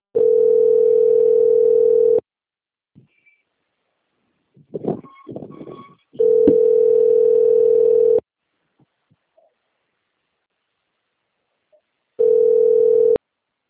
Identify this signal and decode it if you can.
Ringback tone (North America), 440 + 480 Hz.
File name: out- - - - - .4429